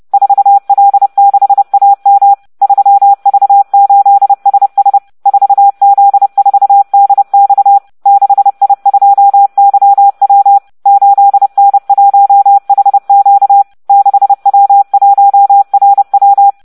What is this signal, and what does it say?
Morse code, heard as a keyed tone, about 30 wpm, 800 Hz. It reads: 4L6AM 3V8SS 4Z4DX 6I2YW 8N1HQ 6W1RW